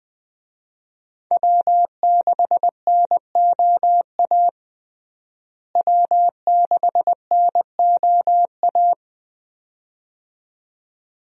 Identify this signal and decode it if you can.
Morse code, heard as a keyed tone, about 20 wpm, 700 Hz. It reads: W6NOA W6NOA
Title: USA Call Signs - 5 Characters - Repeated 2x - 20wpm